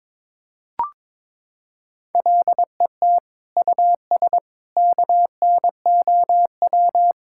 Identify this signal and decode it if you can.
Morse code, heard as a keyed tone, about 22 wpm, 700 Hz. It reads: LET US KNOW
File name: Sentences with 7-Word Phrases - 22wpm